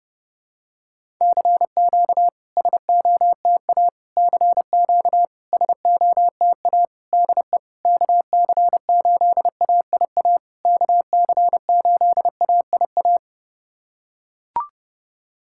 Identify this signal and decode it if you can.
Morse code, heard as a keyed tone, about 30 wpm, 700 Hz. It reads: CQ SOTA CQ SOTA DE KC8AIA KC8AIA
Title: SOTA QSO Practice 30wpm